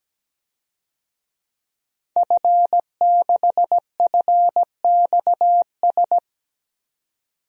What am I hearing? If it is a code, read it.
Morse code, heard as a keyed tone, about 17 wpm, 700 Hz. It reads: F6FXS
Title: DXCC-100 Call Signs - 5 Characters - 17wpm